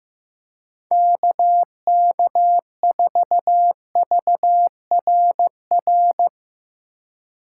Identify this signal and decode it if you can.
Morse code, heard as a keyed tone, about 16 wpm, 700 Hz. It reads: KK4VRR
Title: USA Call Signs - 6 Characters - 3x Character Spacing - 15wpm